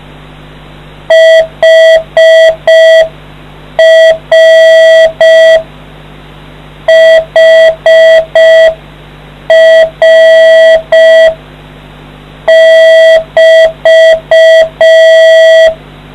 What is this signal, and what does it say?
Morse code, heard as a keyed tone, about 5 wpm, 650 Hz. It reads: HR HR<BT>